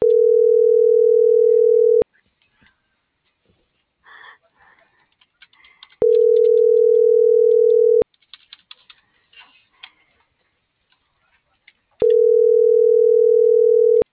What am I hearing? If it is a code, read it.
Ringback tone (North America), 440 + 480 Hz.